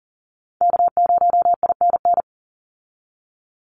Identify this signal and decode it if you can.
Morse code, heard as a keyed tone, about 40 wpm, 700 Hz. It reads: K0INN
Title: Call Signs - 40wpm